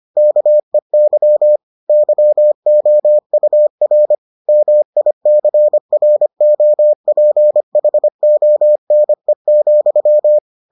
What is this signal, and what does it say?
Morse code, heard as a keyed tone, about 25 wpm, 600 Hz. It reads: KEY YOUR MICROPHONE,